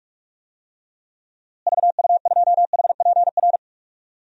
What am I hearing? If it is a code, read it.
Morse code, heard as a keyed tone, about 45 wpm, 700 Hz. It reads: VU2HPR
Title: DXCC-100 Call Signs - 6 Characters - Repeated 2x - 45wpm